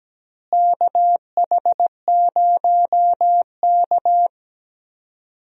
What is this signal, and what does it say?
Morse code, heard as a keyed tone, about 17 wpm, 700 Hz. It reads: KH0K
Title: USA Call Signs - 4 Characters - Repeated 2x - 17wpm